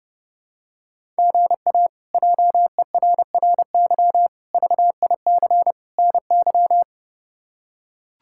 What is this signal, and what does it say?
Morse code, heard as a keyed tone, about 31 wpm, 700 Hz. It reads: GA JERRY VIC NY
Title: K1USN SST QSO Practice - 30wpm